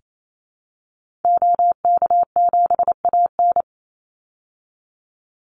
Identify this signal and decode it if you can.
Morse code, heard as a keyed tone, about 28 wpm, 700 Hz. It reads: OK7AN
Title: Call Signs Repeated 2x - 28wpm